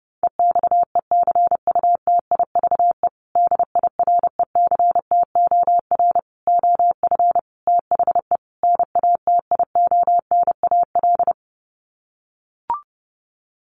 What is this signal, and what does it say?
Morse code, heard as a keyed tone, about 28 wpm, 700 Hz. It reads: EXECUTIVE DIRECTOR OF THE NATIONAL
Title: Sets of 5 Words Encore2 30wpm